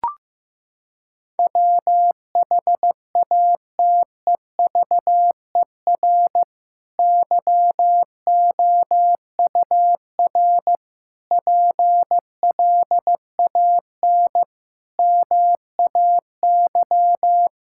Morse code, heard as a keyed tone, about 16 wpm, 700 Hz. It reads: WHATEVER YOUR PLAN MAY
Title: Sentences from Top 1600 Words 15wpm